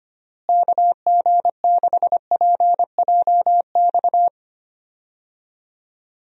Morse code, heard as a keyed tone, about 25 wpm, 700 Hz. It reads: KG6PJX
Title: USA Call Signs - 6 Characters - Repeated 2x - 25wpm